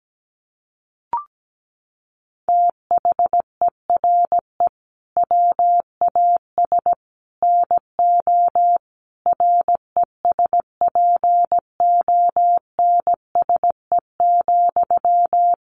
Morse code, heard as a keyed tone, about 17 wpm, 700 Hz. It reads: THERE WAS NO RESPONSE,
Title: Sentences from Top 1400 Words 17wpm